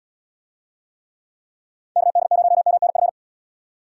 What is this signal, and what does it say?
Morse code, heard as a keyed tone, about 67 wpm, 700 Hz. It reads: KD0MNF